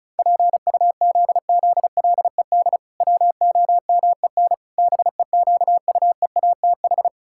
Morse code, heard as a keyed tone, about 34 wpm, 700 Hz. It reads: PUZZLED WOMEN BEQUEATH
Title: Pangram Challenge v2 - 35wpm